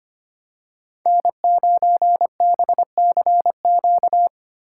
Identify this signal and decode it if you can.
Morse code, heard as a keyed tone, about 25 wpm, 700 Hz. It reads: N9BCQ
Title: USA Call Signs - 5 Characters - 4x Character Spacing - 25wpm